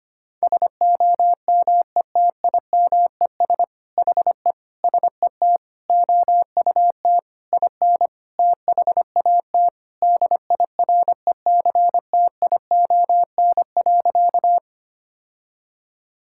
Morse code, heard as a keyed tone, about 25 wpm, 700 Hz. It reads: SOMETIMES HE SET OUT IN THAT DIRECTION.
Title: Sentences from Top 1000 Words - Review Part 5 - 25wpm